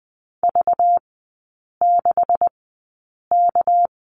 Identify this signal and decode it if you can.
Morse code, heard as a keyed tone, about 20 wpm, spaced as for 8 wpm, 700 Hz. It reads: V6K
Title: 198 Lesson 32e - Letter k - Three Characters 20wpm